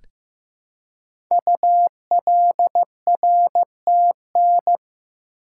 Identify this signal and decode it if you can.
Morse code, heard as a keyed tone, about 15 wpm, 700 Hz. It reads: ULRTN
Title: Code Groups - 5 Letters - 15wpm